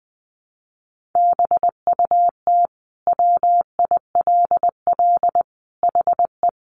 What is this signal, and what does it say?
Morse code, heard as a keyed tone, about 20 wpm, 700 Hz. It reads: BUT WILL HE
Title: Sentences from Top 100 Words 20wpm